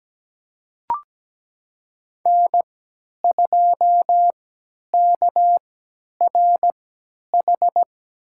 Morse code, heard as a keyed tone, about 17 wpm, spaced as for 10 wpm, 700 Hz. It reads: N2KRH
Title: USA Call Signs - 5 Characters - 3x Character Spacing - 17wpm